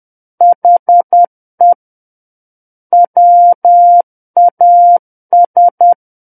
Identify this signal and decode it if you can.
Morse code, heard as a keyed tone, about 10 wpm, 700 Hz. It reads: HE WAS
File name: Book One / 01-Chapter 1 - 10 WPM